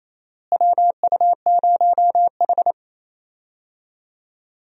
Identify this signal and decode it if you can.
Morse code, heard as a keyed tone, about 28 wpm, 700 Hz. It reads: WU0H
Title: USA Call Signs - 4 Characters - Repeated 2x - 28wpm